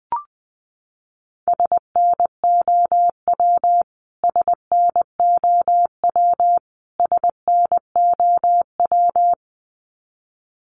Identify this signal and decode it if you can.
Morse code, heard as a keyed tone, about 20 wpm, 700 Hz. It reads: SNOW SNOW SNOW
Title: Top 500 Words Repeated 20wpm